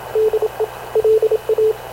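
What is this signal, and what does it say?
Morse code, heard as a keyed tone, about 27 wpm, 430 Hz, with radio noise. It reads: DE LA
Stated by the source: Havana Radio CLA press (PX) broadcast, recorded by Radio Officer Paul Dumesnil aboard mv CNAV QUEST/CZDO.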